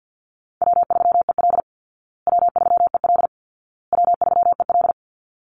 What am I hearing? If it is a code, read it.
Morse code, heard as a keyed tone, about 50 wpm, 700 Hz. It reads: W3EL W3EL W3EL
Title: Call Signs Repeated 2x - 50wpm